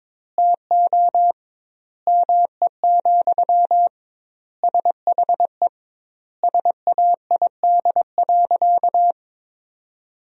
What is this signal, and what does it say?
Morse code, heard as a keyed tone, about 22 wpm, 700 Hz. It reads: TO ME, SHE SAID.
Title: 2x Word-Spacing - Sentences from Top 500 Words - 22wpm